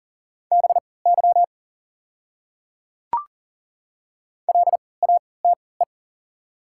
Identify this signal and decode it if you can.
Morse code, heard as a keyed tone, about 40 wpm, spaced as for 23 wpm, 700 Hz. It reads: BY LATE
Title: 3x Character Spacing - Top 300 Words - 40wpm